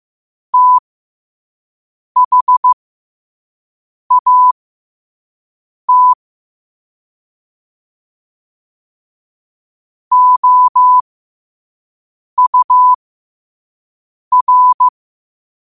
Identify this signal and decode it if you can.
Morse code, heard as a keyed tone, about 15 wpm, spaced as for 5 wpm, 1000 Hz. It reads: THAT OUR
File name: qotdfiles / 5wpm